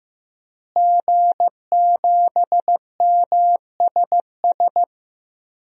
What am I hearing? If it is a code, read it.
Morse code, heard as a keyed tone, about 15 wpm, 700 Hz. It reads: G7MSS
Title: DXCC-100 Call Signs - 5 Characters - Repeated 2x - 15wpm